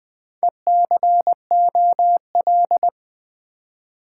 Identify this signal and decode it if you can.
Morse code, heard as a keyed tone, about 20 wpm, 700 Hz. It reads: ECOL
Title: Build A Word - 4 Letters At A Time - 10-Letter Words - 20wpm